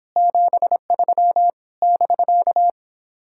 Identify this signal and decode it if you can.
Morse code, heard as a keyed tone, about 26 wpm, 700 Hz. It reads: 73 <BK>